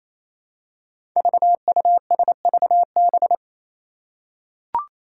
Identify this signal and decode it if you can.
Morse code, heard as a keyed tone, about 28 wpm, 700 Hz. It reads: VUSVB
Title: Five Challenging Letter-Number Combinations 28wpm